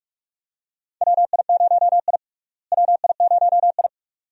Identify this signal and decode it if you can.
Morse code, heard as a keyed tone, about 46 wpm, 700 Hz. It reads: WI0I WI0I
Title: USA Call Signs - 4 Characters - Repeated 2x - 45wpm